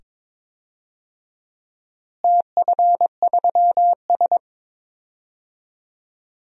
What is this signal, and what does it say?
Morse code, heard as a keyed tone, about 22 wpm, 700 Hz. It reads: TF3S
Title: DXCC-100 Call Signs - 4 Characters - Repeated 2x - 22wpm